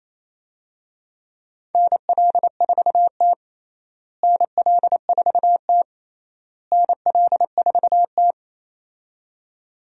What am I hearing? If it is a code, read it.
Morse code, heard as a keyed tone, about 28 wpm, 700 Hz. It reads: NL4T NL4T NL4T
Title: USA Call Signs - 4 Characters - Repeated 3x - 28wpm